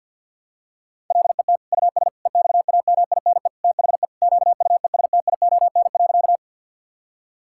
Morse code, heard as a keyed tone, about 50 wpm, 700 Hz. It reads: LET US EXAMINE THE QUESTION.
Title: Sentences from Top 1600 Words 50wpm